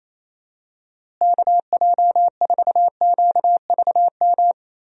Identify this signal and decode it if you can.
Morse code, heard as a keyed tone, about 28 wpm, 700 Hz. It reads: KJ4QVM
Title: USA Call Signs - 6 Characters - Repeated 2x - 28wpm